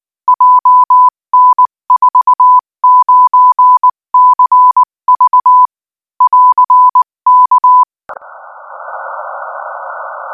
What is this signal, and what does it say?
Morse code, heard as a keyed tone, about 19 wpm, 1000 Hz. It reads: JN49CV <AR>K
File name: IZ4BEH-13-cm-AS